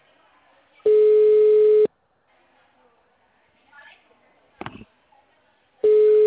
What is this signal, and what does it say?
Ringback tone (Europe, 425 Hz), with speech.
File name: out- -2001- - - .26335